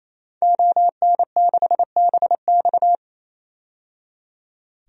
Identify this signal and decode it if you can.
Morse code, heard as a keyed tone, about 28 wpm, 700 Hz. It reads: ON6BX
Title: DXCC-100 Call Signs - 5 Characters - 28wpm